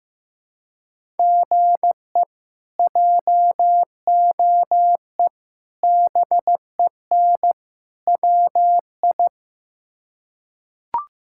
Morse code, heard as a keyed tone, about 15 wpm, 700 Hz. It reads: GE JOE BEN WI
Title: K1USN SST QSO Practice - 15wpm